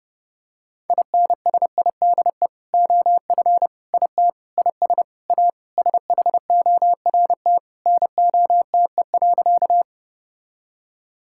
Morse code, heard as a keyed tone, about 32 wpm, 700 Hz. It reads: INSIDE OF IT IS A SHORT NOTE.
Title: Sentences from Top 1000 Words - Review Part 6 - 30wpm